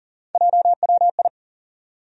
Morse code, heard as a keyed tone, about 40 wpm, 700 Hz. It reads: JWI
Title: USA Call Sign Postfixes - 3 Letters 40wpm